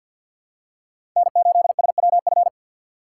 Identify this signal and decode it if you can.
Morse code, heard as a keyed tone, about 49 wpm, 700 Hz. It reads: N8SWF
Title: USA Call Signs - 5 Characters - 4x Character Spacing - 50wpm